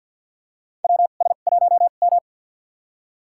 Morse code, heard as a keyed tone, about 50 wpm, spaced as for 36 wpm, 690 Hz. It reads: WS1M
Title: USA Call Signs - 2x Character Spacing - 50wpm